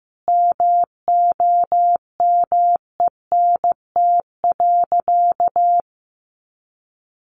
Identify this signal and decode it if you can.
Morse code, heard as a keyed tone, about 15 wpm, 700 Hz. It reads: MOMENT.
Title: Sentences from Top 1000 Words 15wpm